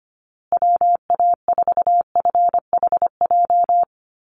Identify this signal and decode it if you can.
Morse code, heard as a keyed tone, about 25 wpm, 700 Hz. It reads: WA4FHJ
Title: USA Call Signs Repeated 2x 25wpm